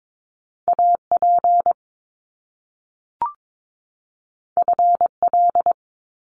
Morse code, heard as a keyed tone, about 22 wpm, 700 Hz. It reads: AP FL
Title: Top Two Letter Combinations in Words 22wpm